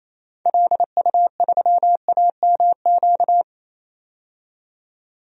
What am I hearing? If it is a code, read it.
Morse code, heard as a keyed tone, about 28 wpm, 700 Hz. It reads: LU3AMQ
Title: DXCC-100 Call Signs - 6 Characters - 28wpm